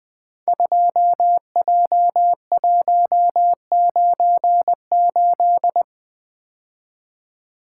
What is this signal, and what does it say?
Morse code, heard as a keyed tone, about 20 wpm, 700 Hz. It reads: 2J198
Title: Code Groups - 5 Letter:Numbers - 20wpm